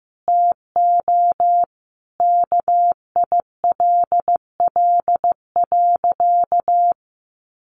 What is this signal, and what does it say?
Morse code, heard as a keyed tone, about 15 wpm, 700 Hz. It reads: TO KILL.
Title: Sentences from Top 1400 Words 15wpm